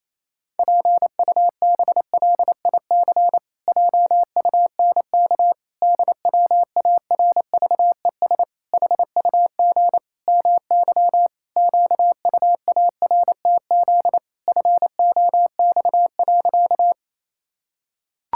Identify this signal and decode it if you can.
Morse code, heard as a keyed tone, about 28 wpm, 700 Hz. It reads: PUBLIC JUNK DWARVES HUG MY QUARTZ FOX.
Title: Pangram Challenge v2 - 28wpm